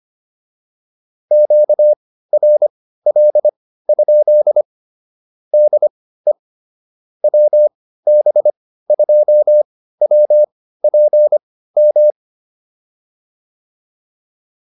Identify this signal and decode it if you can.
Morse code, heard as a keyed tone, about 25 wpm, spaced as for 15 wpm, 600 Hz. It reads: QRL? DE WB2WPM
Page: intermediate-practice-files